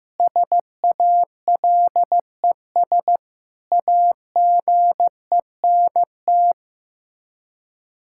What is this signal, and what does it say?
Morse code, heard as a keyed tone, about 15 wpm, 700 Hz. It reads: SALES AGENT
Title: Practice QSO - Exchange 3 - More Info - 15wpm